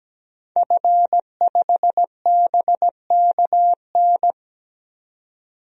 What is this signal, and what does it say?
Morse code, heard as a keyed tone, about 17 wpm, 700 Hz. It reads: F5BKN
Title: DXCC-100 Call Signs - 5 Characters - 17wpm